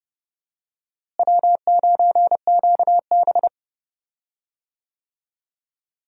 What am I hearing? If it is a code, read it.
Morse code, heard as a keyed tone, about 30 wpm, 700 Hz. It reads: W9QB
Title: USA Call Signs - 4 Characters - Repeated 2x - 30wpm